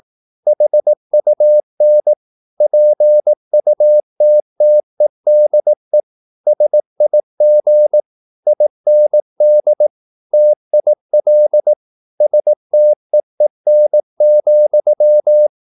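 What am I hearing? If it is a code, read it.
Morse code, heard as a keyed tone, about 18 wpm, 600 Hz. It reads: HUN PUTTEDE SIG IND TIL STEEN,
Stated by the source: Slow (QRS) Morse practice at 18 wpm, CW med Gnister.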